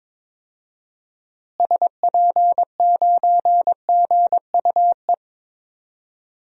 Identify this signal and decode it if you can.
Morse code, heard as a keyed tone, about 22 wpm, 700 Hz. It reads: SP9GUE
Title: DXCC-100 Call Signs - 6 Characters - Repeated 3x - 22wpm